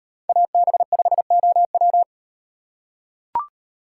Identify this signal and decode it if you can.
Morse code, heard as a keyed tone, about 38 wpm, 700 Hz. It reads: AB5OW